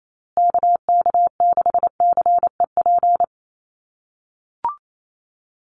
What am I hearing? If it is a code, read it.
Morse code, heard as a keyed tone, about 28 wpm, 700 Hz. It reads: KK6CEP
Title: USA Call Signs Repeated 3x 28wpm